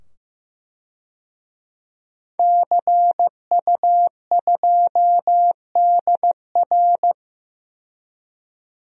Morse code, heard as a keyed tone, about 15 wpm, 700 Hz. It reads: CU2DR
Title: DXCC-100 Call Signs - 5 Characters - 15wpm